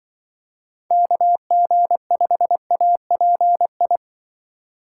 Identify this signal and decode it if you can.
Morse code, heard as a keyed tone, about 25 wpm, 700 Hz. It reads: KG5API